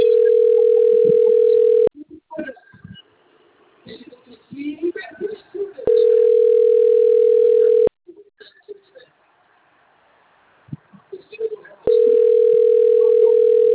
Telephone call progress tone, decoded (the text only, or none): ringback tone